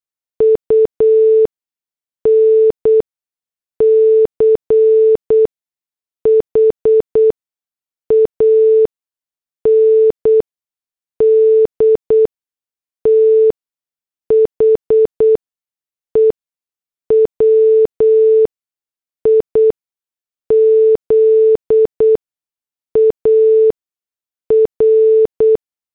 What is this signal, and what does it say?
Morse code, heard as a keyed tone, about 8 wpm, spaced as for 6 wpm, 440 Hz. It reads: UNCH AND THE WIZAR